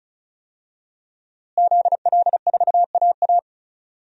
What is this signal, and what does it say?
Morse code, heard as a keyed tone, about 36 wpm, 700 Hz. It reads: ZL4AA
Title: DXCC-100 Call Signs - 5 Characters - Repeated 3x - 35wpm